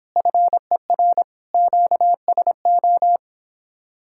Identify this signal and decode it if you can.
Morse code, heard as a keyed tone, about 29 wpm, 700 Hz. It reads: FER QSO